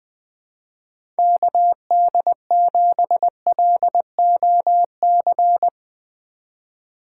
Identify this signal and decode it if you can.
Morse code, heard as a keyed tone, about 20 wpm, 700 Hz. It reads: KD7LOC
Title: USA Call Signs - 6 Characters - 4x Character Spacing - 20wpm